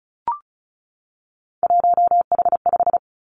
Morse code, heard as a keyed tone, about 36 wpm, 700 Hz. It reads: 1H5
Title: Three Challenging Letter-Number Combinations 35wpm